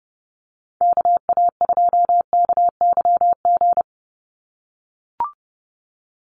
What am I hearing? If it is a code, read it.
Morse code, heard as a keyed tone, about 30 wpm, 700 Hz. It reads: KA2KYG
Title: USA Call Signs 30wpm